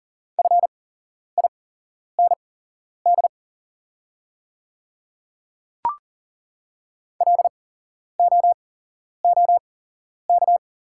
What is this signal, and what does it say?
Morse code, heard as a keyed tone, about 40 wpm, spaced as for 11 wpm, 700 Hz. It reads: FIND LOOK
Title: 8x Character Spacing - Top 100 Words - 40wpm